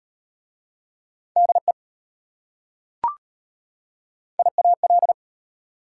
Morse code, heard as a keyed tone, about 33 wpm, 690 Hz. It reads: DE IAL